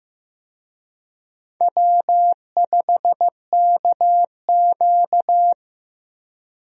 Morse code, heard as a keyed tone, about 15 wpm, 700 Hz. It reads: W5KQ
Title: USA Call Signs - 4 Characters - Repeated 3x - 15wpm